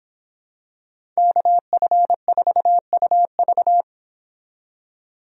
Morse code, heard as a keyed tone, about 26 wpm, 700 Hz. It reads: KF4UV